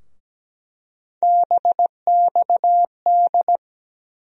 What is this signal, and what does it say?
Morse code, heard as a keyed tone, about 17 wpm, 700 Hz. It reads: BXD
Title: USA Call Sign Postfixes - 3 Letters 17wpm